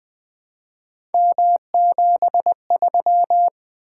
Morse code, heard as a keyed tone, about 20 wpm, 700 Hz. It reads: M73
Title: Three Letter-Number Combinations - 20wpm